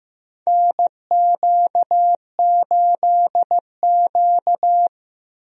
Morse code, heard as a keyed tone, about 15 wpm, 700 Hz. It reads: NQ8Q
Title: USA Call Signs - 4 Characters - Repeated 2x - 15wpm